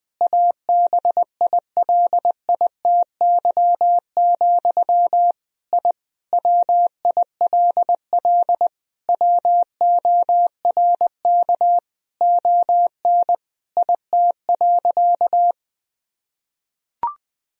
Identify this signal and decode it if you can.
Morse code, heard as a keyed tone, about 21 wpm, 700 Hz. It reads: ABILITY, I WILL WORK ON IT.
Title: Sentences from Top 1000 Words - Review Part 3 - 20wpm